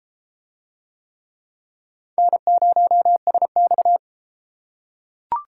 Morse code, heard as a keyed tone, about 32 wpm, 700 Hz. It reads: N0SX